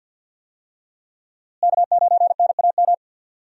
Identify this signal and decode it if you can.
Morse code, heard as a keyed tone, about 51 wpm, 700 Hz. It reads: K9NAM
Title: USA Call Signs - 5 Characters - Repeated 3x - 50wpm